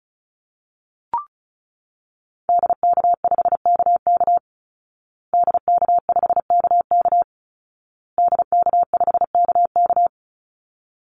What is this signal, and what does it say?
Morse code, heard as a keyed tone, about 34 wpm, 700 Hz. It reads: DK5KK DK5KK DK5KK
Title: Call Signs Repeated 2x - 35wpm